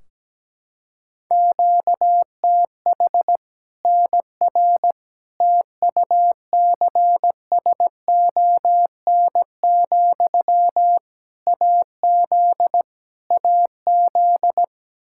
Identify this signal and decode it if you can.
Morse code, heard as a keyed tone, about 17 wpm, 700 Hz. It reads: QTH NR TUCSON, AZ AZ
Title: Practice QSO - Exchange 1 - Beginning - 17wpm